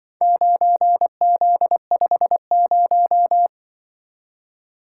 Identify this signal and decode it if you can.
Morse code, heard as a keyed tone, about 25 wpm, 700 Hz. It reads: 9Z50
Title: Speed-Racing - Four Letter-Number Combinations - 20wpm